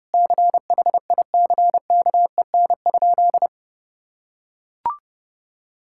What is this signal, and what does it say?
Morse code, heard as a keyed tone, about 30 wpm, 700 Hz. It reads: CHICKEN?
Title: Sentences from Top 1000 Words - Review Part 5 - 30wpm